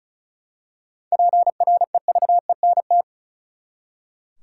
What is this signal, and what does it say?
Morse code, heard as a keyed tone, about 34 wpm, 700 Hz. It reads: PREVENT
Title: Seven Letter Words - 35wpm